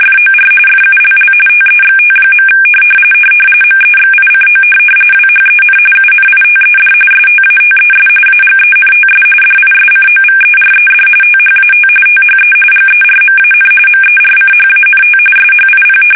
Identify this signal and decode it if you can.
RTTY, about 45.45 baud. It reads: FOR FULL TEXT. MEXICO WHEN UNITED STATES AGENTS DISCOVERED THAT THE GERMAN MERCHANT SHIP
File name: fsk850